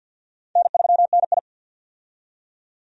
Morse code, heard as a keyed tone, about 52 wpm, 700 Hz. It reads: N3NI
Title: USA Call Signs - 4 Characters - 50wpm